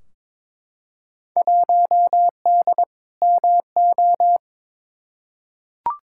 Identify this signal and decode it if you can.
Morse code, heard as a keyed tone, about 22 wpm, 700 Hz. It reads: 1D MO
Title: Field Day Exchanges - 22wpm